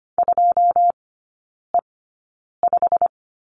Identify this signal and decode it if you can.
Morse code, heard as a keyed tone, about 25 wpm, spaced as for 9 wpm, 700 Hz. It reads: 2E5
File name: Lesson 15e - Number 5 - Three Characters 25wpm